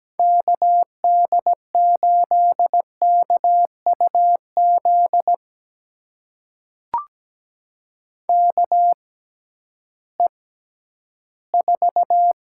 Morse code, heard as a keyed tone, about 17 wpm, spaced as for 11 wpm, 700 Hz. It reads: KD8KUZ KE4
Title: USA Call Signs - 6 Characters - 6x Character Spacing - 17wpm